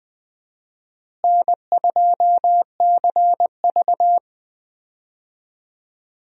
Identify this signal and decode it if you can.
Morse code, heard as a keyed tone, about 20 wpm, 700 Hz. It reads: N2CV
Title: USA Call Signs - 4 Characters - Repeated 2x - 20wpm